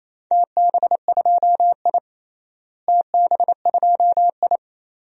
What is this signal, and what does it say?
Morse code, heard as a keyed tone, about 28 wpm, 700 Hz. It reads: TB2I TB2I
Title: DXCC-100 Call Signs - 4 Characters - Repeated 3x - 28wpm